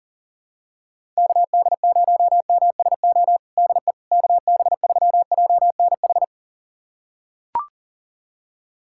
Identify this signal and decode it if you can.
Morse code, heard as a keyed tone, about 40 wpm, 700 Hz. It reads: KD0MSO DE KB3JNH
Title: USA Callsign DE Callsign - 6 Characters - 40wpm